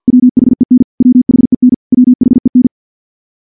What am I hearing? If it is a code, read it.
Morse code, heard as a keyed tone, about 46 wpm, 250 Hz. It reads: WHEN WHEN WHEN
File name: morse / morse-training-50WPM